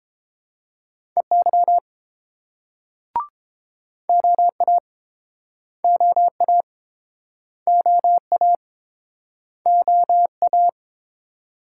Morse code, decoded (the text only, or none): EY OA OA OA OA